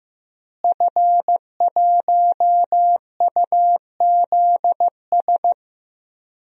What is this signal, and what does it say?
Morse code, heard as a keyed tone, about 15 wpm, 700 Hz. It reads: F1UZS
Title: DXCC-100 Call Signs - 5 Characters - Repeated 3x - 15wpm